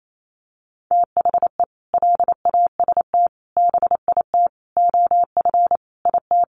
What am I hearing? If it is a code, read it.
Morse code, heard as a keyed tone, about 28 wpm, 700 Hz. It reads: THE LAST BIT OF IT
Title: Sentences from Top 900 Words 28wpm